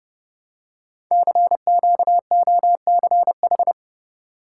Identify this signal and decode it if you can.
Morse code, heard as a keyed tone, about 28 wpm, 700 Hz. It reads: CQOCH
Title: Code Groups - 5 Letter:Numbers - 30wpm